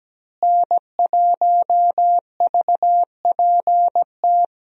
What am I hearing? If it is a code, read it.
Morse code, heard as a keyed tone, about 17 wpm, 700 Hz. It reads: N1VPT
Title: USA Call Signs - 5 Characters - Repeated 3x - 17wpm